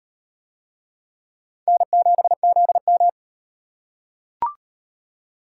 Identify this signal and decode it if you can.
Morse code, heard as a keyed tone, about 38 wpm, 700 Hz. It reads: N7ZM